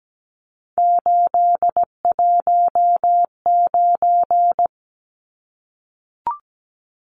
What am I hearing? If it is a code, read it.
Morse code, heard as a keyed tone, about 17 wpm, 700 Hz. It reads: 819
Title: Numbers - 3-Digit 17wpm